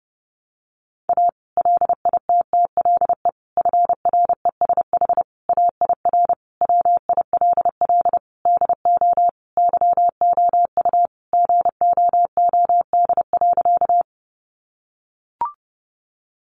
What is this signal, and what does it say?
Morse code, heard as a keyed tone, about 30 wpm, 700 Hz. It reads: A LITTLE FRESH AIR WILL DO YOU GOOD.
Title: Sentences from Top 1200 Words 30wpm